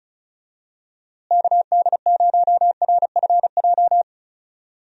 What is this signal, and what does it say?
Morse code, heard as a keyed tone, about 34 wpm, 700 Hz. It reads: KD0RFJ
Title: USA Call Signs - 6 Characters - Repeated 3x - 35wpm